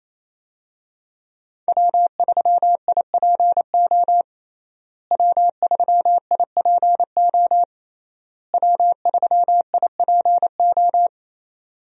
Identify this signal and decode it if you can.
Morse code, heard as a keyed tone, about 28 wpm, 700 Hz. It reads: W3IPO W3IPO W3IPO
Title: USA Call Signs - 5 Characters - Repeated 3x - 28wpm